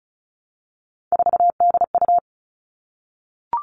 Morse code, heard as a keyed tone, about 34 wpm, 700 Hz. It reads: 4DU
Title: All Cut Numbers - 3-Digit 35wpm